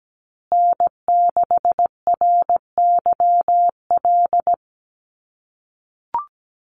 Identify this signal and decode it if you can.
Morse code, heard as a keyed tone, about 17 wpm, 700 Hz. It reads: N6RYL
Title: USA Call Signs Repeated 3x 17wpm